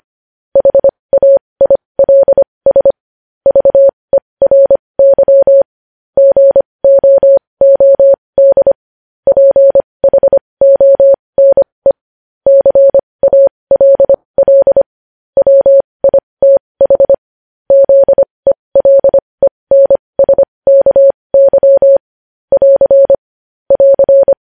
Morse code, heard as a keyed tone, about 25 wpm, spaced as for 20 wpm, 550 Hz. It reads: HAILS VERY GOOD PHONE CALL WITH ZELENSKY <AR> <AR>
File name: podcast / News headlines-1x-25x20